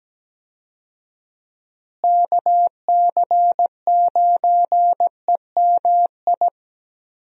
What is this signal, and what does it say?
Morse code, heard as a keyed tone, about 17 wpm, 700 Hz. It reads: KC9EMI
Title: USA Call Signs - 6 Characters - Repeated 2x - 17wpm